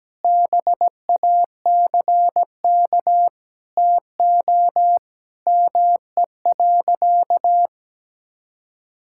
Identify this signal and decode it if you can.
Morse code, heard as a keyed tone, about 17 wpm, 700 Hz. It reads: BACK TO ME.
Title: Sentences from Top 1000 Words - Review Part 6 - 17wpm